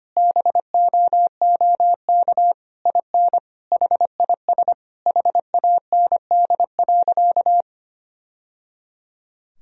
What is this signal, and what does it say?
Morse code, heard as a keyed tone, about 25 wpm, 700 Hz. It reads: BOOK IN HIS HAND.
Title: Sentences from Top 1000 Words - Review Part 1 - 25wpm